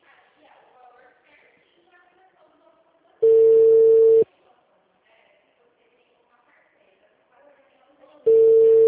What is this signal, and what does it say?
Ringback tone (China, 450 Hz), with speech.
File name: out- -2006- - - .26